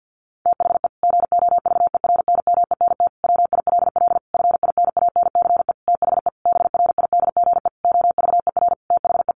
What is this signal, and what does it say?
Morse code, heard as a keyed tone, about 50 wpm, 700 Hz. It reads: THE GOVERNMENT WILL FINANCE THE BRIDGE OVER THE
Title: Sentences from Top 1500 Words 50wpm